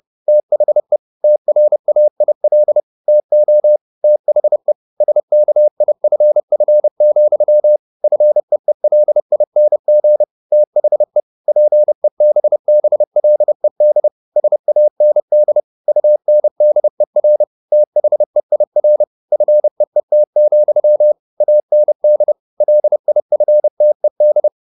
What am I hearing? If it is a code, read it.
Morse code, heard as a keyed tone, about 30 wpm, 600 Hz. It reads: THE TRAIL TO THE SKIFF, FEELING THE PEBBLED SAND UNDER THEIR FEET, AND LIFTED
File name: the old man and the sea / the old man and the sea 1 30wpm